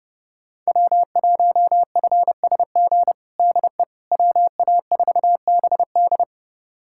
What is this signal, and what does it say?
Morse code, heard as a keyed tone, about 30 wpm, 700 Hz. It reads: W1FSG DE WA4BD
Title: USA Callsign DE Callsign - 5 Characters - 30wpm